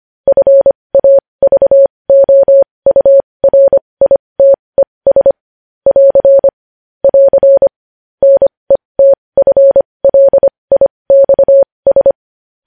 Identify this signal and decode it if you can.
Morse code, heard as a keyed tone, about 25 wpm, spaced as for 20 wpm, 550 Hz. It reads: FAVOURITES <AR> <AR> NETFLIXS
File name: podcast / News headlines-1x-25x20